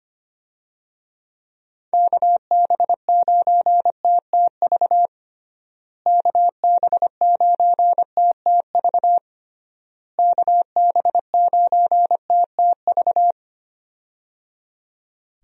Morse code, heard as a keyed tone, about 25 wpm, 700 Hz. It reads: KB9TTV KB9TTV KB9TTV
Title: USA Call Signs - 6 Characters - Repeated 3x - 25wpm